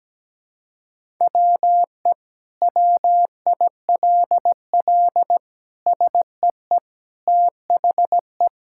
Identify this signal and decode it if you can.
Morse code, heard as a keyed tone, about 17 wpm, 700 Hz. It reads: WE WILL SEE THE
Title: Sentences from Top 1000 Words - Review Part 5 - 17wpm